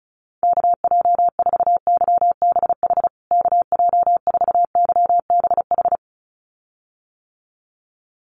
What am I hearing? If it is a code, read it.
Morse code, heard as a keyed tone, about 34 wpm, 700 Hz. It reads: KJ4YBH KJ4YBH
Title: USA Call Signs Repeated 2x 35wpm